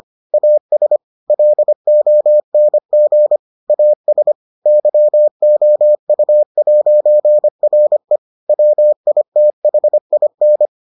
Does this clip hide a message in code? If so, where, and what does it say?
Yes: Morse code from 0.3 to 10.7 s, heard as a keyed tone, about 25 wpm, 600 Hz. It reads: AS LONG AS YOU'RE WITHIN